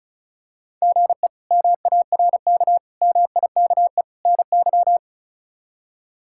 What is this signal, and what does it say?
Morse code, heard as a keyed tone, about 36 wpm, 700 Hz. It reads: GE MARK MIKE NY
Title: K1USN SST QSO Practice - 35wpm